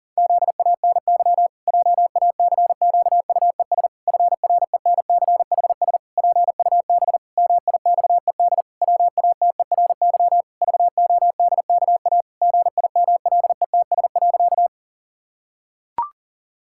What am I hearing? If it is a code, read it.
Morse code, heard as a keyed tone, about 40 wpm, 700 Hz. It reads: ZANY JACQUES FRENCHS PUB MIXED WATERY VODKA GIMLETS.
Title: Pangram Challenge v2 - 40wpm